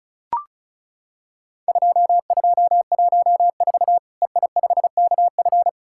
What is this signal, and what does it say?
Morse code, heard as a keyed tone, about 36 wpm, 700 Hz. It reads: 2214 EI5KF
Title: Worked All Europe DX Contest - QTC Exchange - 35wpm